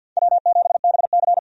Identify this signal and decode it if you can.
Morse code, heard as a keyed tone, about 52 wpm, 700 Hz. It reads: W7BC
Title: USA Call Signs - 4 Characters - 50wpm